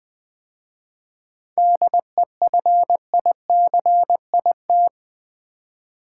Morse code, heard as a keyed tone, about 21 wpm, 700 Hz. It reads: DEFICIT
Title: 7 Letter Words - First 3 Letters Given - 20wpm